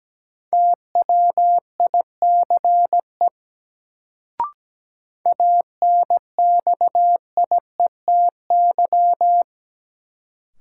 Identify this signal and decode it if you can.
Morse code, heard as a keyed tone, about 17 wpm, 700 Hz. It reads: TWICE ANXIETY
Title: Sentences from Top 1600 Words - New Word Drill - 17wpm